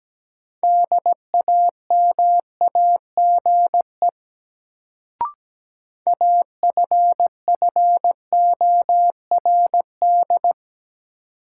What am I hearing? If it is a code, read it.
Morse code, heard as a keyed tone, about 17 wpm, 700 Hz. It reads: DAMAGE AFFORD
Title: Sentences from Top 1100 Words - New Word Drill - 17wpm